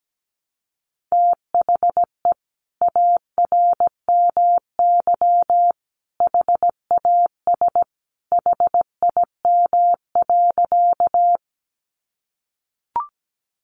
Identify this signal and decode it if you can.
Morse code, heard as a keyed tone, about 17 wpm, 700 Hz. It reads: THE ARMY HAS HIM.
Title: Sentences from Top 800 Words 17wpm